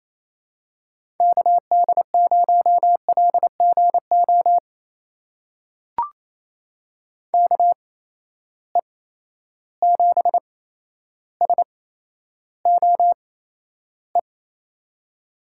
Morse code, heard as a keyed tone, about 28 wpm, spaced as for 12 wpm, 700 Hz. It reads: KD0LGO KE7SOE
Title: USA Call Signs - 6 Characters - 8x Character Spacing - 28wpm